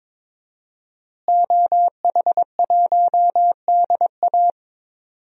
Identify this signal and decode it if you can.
Morse code, heard as a keyed tone, about 22 wpm, 700 Hz. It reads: OH1DA
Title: DXCC-100 Call Signs - 5 Characters - Repeated 2x - 22wpm